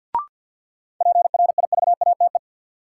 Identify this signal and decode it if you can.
Morse code, heard as a keyed tone, about 49 wpm, 700 Hz. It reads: PRIVATE
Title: Sentences from Top 1000 Words - New Word Drill - 50wpm